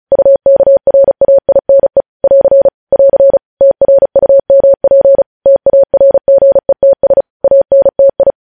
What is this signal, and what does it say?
Morse code, heard as a keyed tone, about 36 wpm, 550 Hz. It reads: UKRAINE <AR> <AR> TRUMP TARGETS ANTI
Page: News headlines in Morse code at 35 wpm